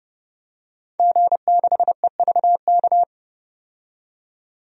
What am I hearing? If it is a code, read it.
Morse code, heard as a keyed tone, about 32 wpm, 700 Hz. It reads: G6EVK
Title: DXCC-100 Call Signs - 5 Characters - 30wpm